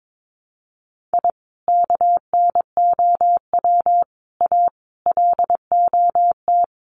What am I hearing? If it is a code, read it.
Morse code, heard as a keyed tone, about 22 wpm, 700 Hz. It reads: I KNOW A LOT
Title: Sets of 5 Words Encore 22wpm